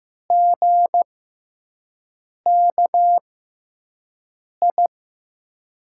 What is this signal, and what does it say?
Morse code, heard as a keyed tone, about 15 wpm, spaced as for 5 wpm, 700 Hz. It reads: GKI
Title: USA Call Signs - 6 Characters - 6x Character Spacing - 15wpm